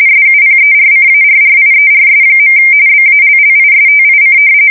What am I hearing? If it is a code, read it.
RTTY, about 45.45 baud. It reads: SUPERCARRIERS. BALLISTIC
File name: fsk170 / usnavy5